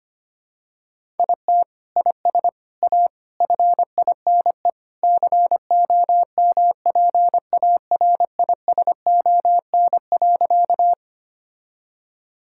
Morse code, heard as a keyed tone, about 25 wpm, 700 Hz. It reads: IT IS A FINE COMPARISON.
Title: Sentences from Top 1600 Words 25wpm